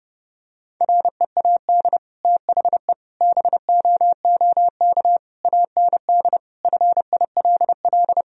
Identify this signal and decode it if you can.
Morse code, heard as a keyed tone, about 30 wpm, 700 Hz. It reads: READ THE BOOK AND FILL
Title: Sentences from Top 1000 Words - Review Part 3 - 30wpm